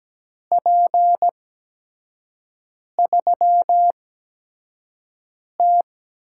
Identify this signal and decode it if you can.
Morse code, heard as a keyed tone, about 17 wpm, spaced as for 5 wpm, 700 Hz. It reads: P3T
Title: USA Call Signs - 5 Characters - 8x Character Spacing - 17wpm